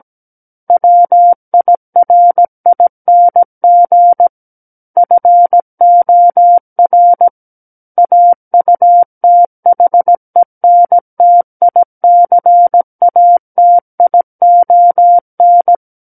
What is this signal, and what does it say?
Morse code, heard as a keyed tone, about 17 wpm, 700 Hz. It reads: WIRING FOR AUTHENTICATION
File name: Book One / 08-Chapter 8 - 17 WPM